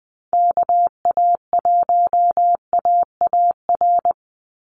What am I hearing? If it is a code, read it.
Morse code, heard as a keyed tone, about 20 wpm, 700 Hz. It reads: KA1AAR
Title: USA Call Signs 20wpm